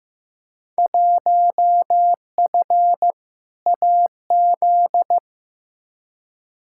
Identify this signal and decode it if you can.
Morse code, heard as a keyed tone, about 15 wpm, 700 Hz. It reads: 1F AZ
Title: Field Day Exchanges - 15wpm